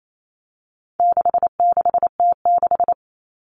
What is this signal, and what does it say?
Morse code, heard as a keyed tone, about 28 wpm, 700 Hz. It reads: 66T6
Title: Common Cut Numbers - 4-Digit 28wpm